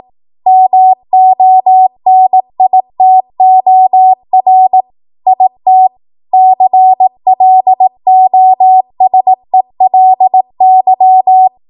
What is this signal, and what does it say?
Morse code, heard as a keyed tone, about 18 wpm, 750 Hz. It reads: MONITOR IT CLOSELY
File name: Morse / 18WPM